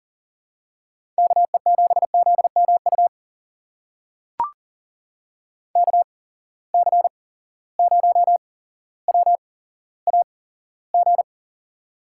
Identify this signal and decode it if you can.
Morse code, heard as a keyed tone, about 41 wpm, spaced as for 17 wpm, 700 Hz. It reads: KE7ZMU KC0WAG
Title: USA Call Signs - 8x Character Spacing - 40wpm